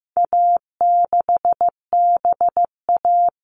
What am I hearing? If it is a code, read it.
Morse code, heard as a keyed tone, about 15 wpm, 700 Hz. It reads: A6BA
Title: All Cut Numbers - 4-Digit 15wpm